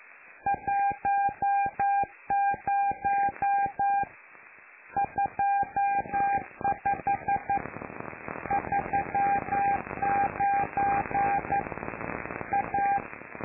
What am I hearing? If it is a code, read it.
Morse code, heard as a keyed tone, about 11 wpm, 800 Hz. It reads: 10 25 39 A
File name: lu4aao / Globo Amsat